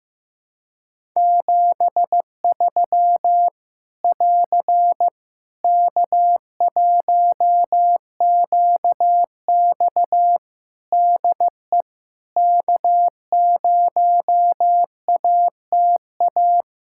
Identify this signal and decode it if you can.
Morse code, heard as a keyed tone, about 16 wpm, 700 Hz. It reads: 73 <AR> K1QX DE K0ATA
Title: Practice QSO - Exchange 4 - Ending - 15wpm